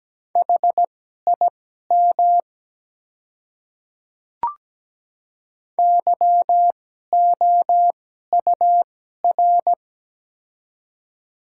Morse code, heard as a keyed tone, about 17 wpm, spaced as for 12 wpm, 700 Hz. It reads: HIM YOUR
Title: 2x Character Spacing - Top 100 Words - 17wpm